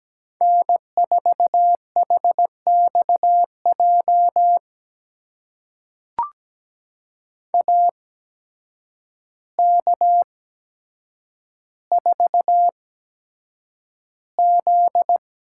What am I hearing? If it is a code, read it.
Morse code, heard as a keyed tone, about 17 wpm, spaced as for 8 wpm, 700 Hz. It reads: N4HXJ AK4Z
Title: USA Call Signs - 5 Characters - 8x Character Spacing - 17wpm